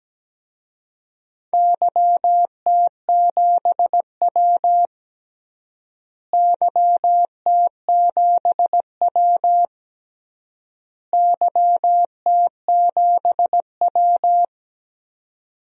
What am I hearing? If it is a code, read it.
Morse code, heard as a keyed tone, about 17 wpm, 700 Hz. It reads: YT7W YT7W YT7W
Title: DXCC-100 Call Signs - 4 Characters - Repeated 3x - 17wpm